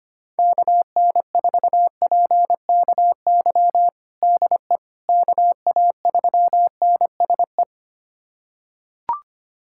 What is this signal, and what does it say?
Morse code, heard as a keyed tone, about 25 wpm, 700 Hz. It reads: KN4PKY DE KA3NSE
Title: USA Callsign DE Callsign - 6 Characters - 25wpm